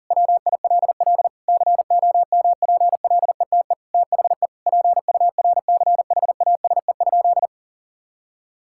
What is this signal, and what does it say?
Morse code, heard as a keyed tone, about 40 wpm, 700 Hz. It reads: WILL COMPLETE THE PURCHASE?
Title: Sentences from Top 1000 Words - Review Part 5 - 40wpm